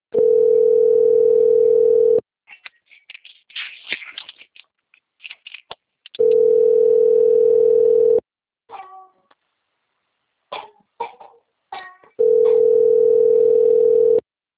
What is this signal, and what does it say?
Ringback tone (North America), 440 + 480 Hz.